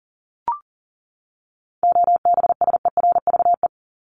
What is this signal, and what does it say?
Morse code, heard as a keyed tone, about 38 wpm, 700 Hz. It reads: OBSERVE
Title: ED Suffix - Top 100 Words with Base Words 40wpm